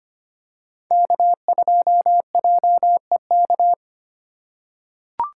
Morse code, heard as a keyed tone, about 25 wpm, 700 Hz. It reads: K2JEK
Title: USA Call Signs - 4x Character Spacing - 25wpm